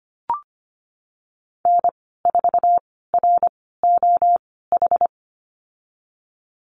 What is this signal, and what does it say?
Morse code, heard as a keyed tone, about 25 wpm, spaced as for 16 wpm, 700 Hz. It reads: N4ROH
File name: Lesson 15g - Number 5 - Call Signs 25wpm